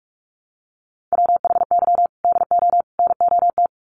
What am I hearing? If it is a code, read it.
Morse code, heard as a keyed tone, about 44 wpm, 700 Hz. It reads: WHY DO NOT
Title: Sets of 3 Words 45wpm